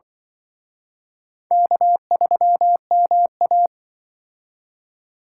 Morse code, heard as a keyed tone, about 25 wpm, 700 Hz. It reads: K3MA
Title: Speed Racing - USA Call Signs 20wpm